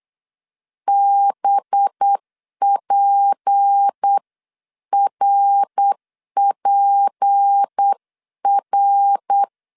Morse code, heard as a keyed tone, about 8 wpm, 800 Hz. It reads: BPRPR
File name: CW / DET / 1 6 1500C S C 8wpm 5219